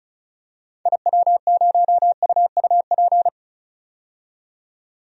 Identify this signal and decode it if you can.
Morse code, heard as a keyed tone, about 34 wpm, 700 Hz. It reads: IW0UUP
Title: DXCC-100 Call Signs - 6 Characters - 35wpm